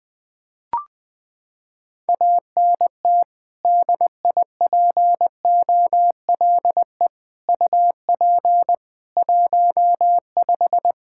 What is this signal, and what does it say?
Morse code, heard as a keyed tone, about 21 wpm, 700 Hz. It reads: ANT DIPOLE UP 15
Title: Practice QSO - Exchange 2 - Info - 20wpm